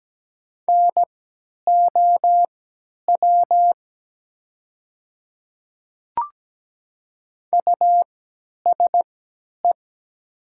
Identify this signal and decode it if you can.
Morse code, heard as a keyed tone, about 17 wpm, spaced as for 10 wpm, 700 Hz. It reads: NOW USE
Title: 3x Character Spacing - Top 200 Words - 17wpm